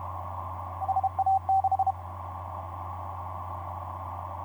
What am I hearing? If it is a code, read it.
Morse code, heard as a keyed tone, about 30 wpm, 770 Hz, with noise. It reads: SA6
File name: 8r1rpn dl8mld 15m cw etc 27oct2010 1127